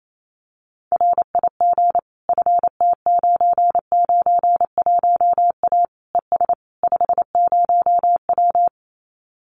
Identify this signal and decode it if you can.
Morse code, heard as a keyed tone, about 28 wpm, 700 Hz. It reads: RIG FT991A ES 50W
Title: Common QSO Phrases Encore 28wpm